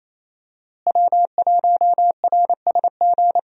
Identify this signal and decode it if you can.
Morse code, heard as a keyed tone, about 28 wpm, 700 Hz. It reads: W1RSG
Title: USA Call Signs - 5 Characters - Repeated 3x - 28wpm